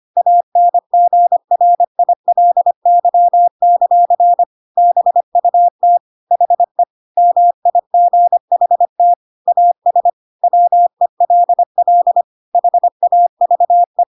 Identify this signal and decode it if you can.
Morse code, heard as a keyed tone, about 25 wpm, 700 Hz. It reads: ANGRILY; BUT HE MIGHT AS WELL HAVE
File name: Pinocchio / Pinocchio ch3 25 25 ik1hge